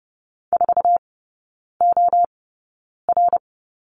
Morse code, heard as a keyed tone, about 30 wpm, spaced as for 9 wpm, 700 Hz. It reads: 4OR
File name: course / 30:10wpm / 087 Lesson 15e - Number 5 - Three Characters 30wpm